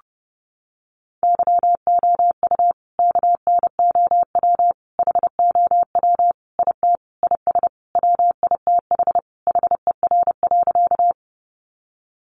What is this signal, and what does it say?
Morse code, heard as a keyed tone, about 30 wpm, 700 Hz. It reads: YOU KNOW HOW IT IS WITH HER.
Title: Sentences from Top 100 Words 30wpm